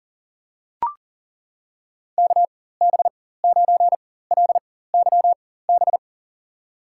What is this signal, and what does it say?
Morse code, heard as a keyed tone, about 40 wpm, spaced as for 19 wpm, 700 Hz. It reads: KB9LYB
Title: USA Call Signs - 6 Characters - 4x Character Spacing - 40wpm